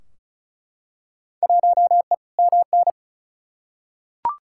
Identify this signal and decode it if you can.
Morse code, heard as a keyed tone, about 36 wpm, 700 Hz. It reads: 1E MN
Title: Field Day Exchanges - 35wpm